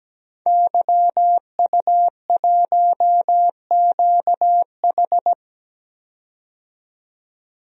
Morse code, heard as a keyed tone, about 17 wpm, 700 Hz. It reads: YU1QH
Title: DXCC-100 Call Signs - 5 Characters - 17wpm